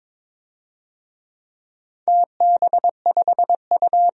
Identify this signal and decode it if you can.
Morse code, heard as a keyed tone, about 22 wpm, 700 Hz. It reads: TB5U
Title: DXCC-100 Call Signs - 4 Characters - 22wpm